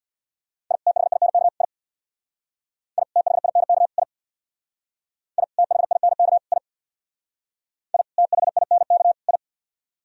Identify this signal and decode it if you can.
Morse code, heard as a keyed tone, about 62 wpm, 700 Hz. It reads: I THINK I I THINK I I THINK I I THINK I
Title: Speed-Racing - Sets of 3 Words 50wpm